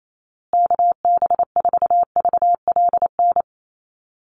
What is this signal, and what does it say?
Morse code, heard as a keyed tone, about 28 wpm, 700 Hz. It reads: KB4VLN
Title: USA Call Signs Repeated 2x 28wpm